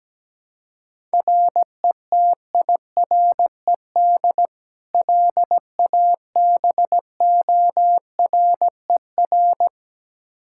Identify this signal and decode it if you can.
Morse code, heard as a keyed tone, about 17 wpm, 700 Hz. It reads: RETIRED LABORER
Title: Practice QSO - Exchange 3 - More Info - 17wpm